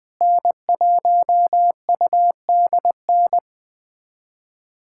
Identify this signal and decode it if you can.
Morse code, heard as a keyed tone, about 20 wpm, 700 Hz. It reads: N1UDN
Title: USA Callsign DE Callsign 20wpm